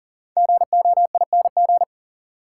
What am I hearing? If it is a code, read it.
Morse code, heard as a keyed tone, about 40 wpm, 700 Hz. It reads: GOING
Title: Sentences from Top 500 Words - New Word Drill - 40wpm